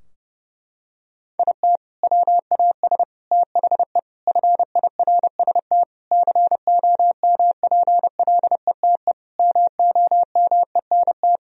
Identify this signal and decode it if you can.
Morse code, heard as a keyed tone, about 30 wpm, 700 Hz. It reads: IT WAS THE FIRST COMPLETE MOMENT
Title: Sentences from Top 1000 Words - Review Part 2 - 30wpm